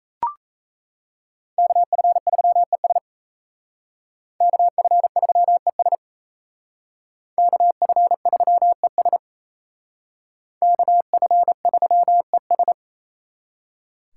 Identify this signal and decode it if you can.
Morse code, heard as a keyed tone, about 36 wpm, 700 Hz. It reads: KF3ES KF3ES KF3ES KF3ES